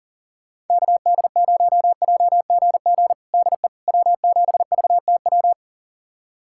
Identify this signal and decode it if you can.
Morse code, heard as a keyed tone, about 40 wpm, 700 Hz. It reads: KD0JGG DE W7VTW
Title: USA Callsign DE Callsign 40wpm